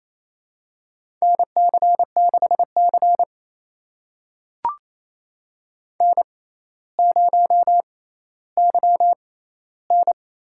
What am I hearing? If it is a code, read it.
Morse code, heard as a keyed tone, about 28 wpm, spaced as for 14 wpm, 700 Hz. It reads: NC6C N0YN
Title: USA Call Signs - 4 Characters - 6x Character Spacing - 28wpm